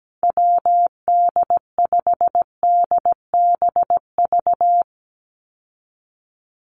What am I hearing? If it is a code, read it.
Morse code, heard as a keyed tone, about 17 wpm, 700 Hz. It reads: WD5DBV
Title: Call Signs - 17wpm